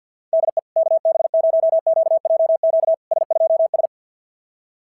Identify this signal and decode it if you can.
Morse code, heard as a keyed tone, about 50 wpm, 690 Hz. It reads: DE KB0QJQ S2S
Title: SOTA QSO Practice 50wpm